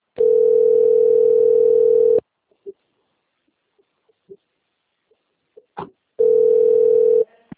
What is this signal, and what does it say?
Ringback tone (North America), 440 + 480 Hz.